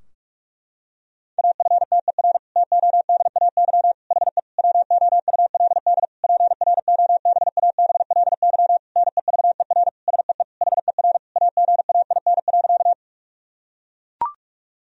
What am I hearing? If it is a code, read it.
Morse code, heard as a keyed tone, about 41 wpm, 700 Hz. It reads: AFTER TODAY HE WOULD PROBABLY NEVER SEE HER AGAIN.
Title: Sentences from Top 1000 Words - Review Part 4 - 45wpm